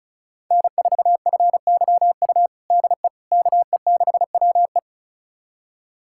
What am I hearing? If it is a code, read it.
Morse code, heard as a keyed tone, about 36 wpm, 700 Hz. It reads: N4FYU DE KE6WE
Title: USA Callsign DE Callsign - 5 Characters - 35wpm